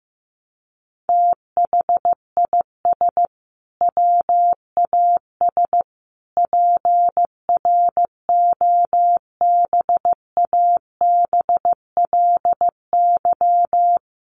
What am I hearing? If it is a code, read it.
Morse code, heard as a keyed tone, about 15 wpm, 700 Hz. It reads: THIS WAS PROBABLY
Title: Sentences from Top 700 Words 15wpm